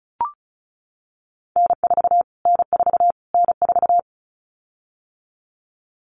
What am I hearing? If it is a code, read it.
Morse code, heard as a keyed tone, about 34 wpm, 700 Hz. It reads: N4 N4 N4
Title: Callsign Prefixes for Top 25 DXCC - Repeated 3x - 35wpm